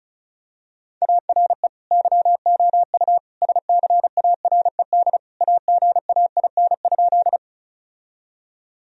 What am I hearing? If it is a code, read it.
Morse code, heard as a keyed tone, about 36 wpm, 700 Hz. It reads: ARE YOU SCARED AGAIN?
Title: Sentences from Top 1000 Words - Review Part 4 - 35wpm